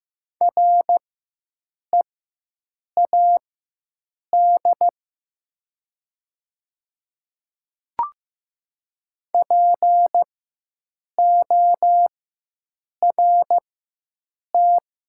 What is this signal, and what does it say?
Morse code, heard as a keyed tone, about 15 wpm, spaced as for 7 wpm, 700 Hz. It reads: READ PORT
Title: 4x Character Spacing - Top 200 Words - 15wpm